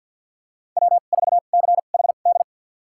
Morse code, heard as a keyed tone, about 52 wpm, spaced as for 37 wpm, 700 Hz. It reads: W4CHD
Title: USA Call Signs - 5 Characters - 2x Character Spacing - 50wpm